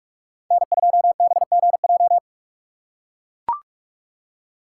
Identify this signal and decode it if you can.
Morse code, heard as a keyed tone, about 43 wpm, 700 Hz. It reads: N2BGJ